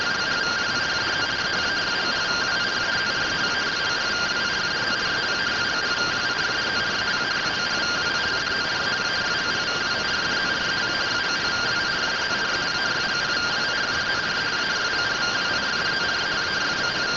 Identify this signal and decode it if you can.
Two-tone teleprinter signal, about 170 Hz shift.